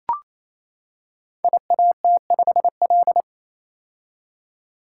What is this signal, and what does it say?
Morse code, heard as a keyed tone, about 29 wpm, 700 Hz. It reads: IAT5L
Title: Code Groups - 5 Letter:Numbers - 28wpm